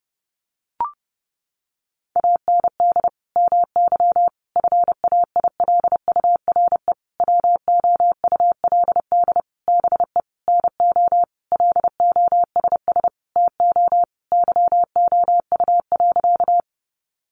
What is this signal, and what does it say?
Morse code, heard as a keyed tone, about 30 wpm, 700 Hz. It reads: AND MY FAILURE WOULD BE NO LOSS TO YOU.
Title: Sentences from Top 1400 Words 30wpm